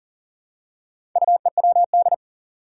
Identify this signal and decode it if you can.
Morse code, heard as a keyed tone, about 40 wpm, 700 Hz. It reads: UEWD
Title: Four Challenging Letter-Number Combinations 40wpm